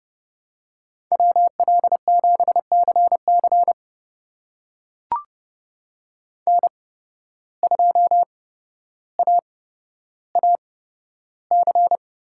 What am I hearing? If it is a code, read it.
Morse code, heard as a keyed tone, about 30 wpm, spaced as for 13 wpm, 700 Hz. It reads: WL7CC N2AAC
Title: USA Call Signs - 5 Characters - 8x Character Spacing - 30wpm